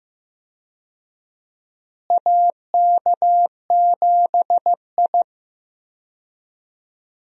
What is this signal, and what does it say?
Morse code, heard as a keyed tone, about 15 wpm, 700 Hz. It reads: AK7I
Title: USA Call Signs - 4 Characters - Repeated 2x - 15wpm